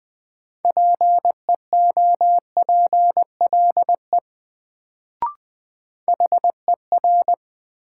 Morse code, heard as a keyed tone, about 21 wpm, 700 Hz. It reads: PEOPLE HER
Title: Sets of 2 Words - New Word Drill - 20wpm